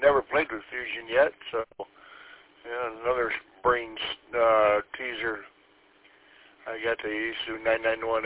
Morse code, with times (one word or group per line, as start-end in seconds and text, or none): none